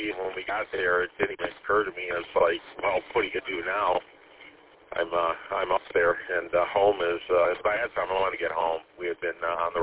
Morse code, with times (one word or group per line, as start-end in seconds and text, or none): none